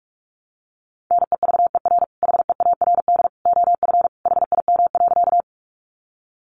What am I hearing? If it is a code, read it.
Morse code, heard as a keyed tone, about 44 wpm, 700 Hz. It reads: NEVER HEARD OF HIM.
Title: Sentences from Top 400 Words 45wpm